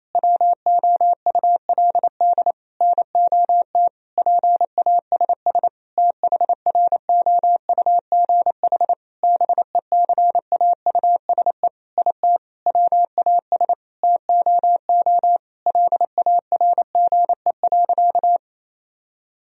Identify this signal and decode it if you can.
Morse code, heard as a keyed tone, about 28 wpm, 700 Hz. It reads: WOULD NOT PASS THROUGH BECAUSE IT WAS TOO LARGE.
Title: Sentences from Top 1000 Words - Review Part 1 - 28wpm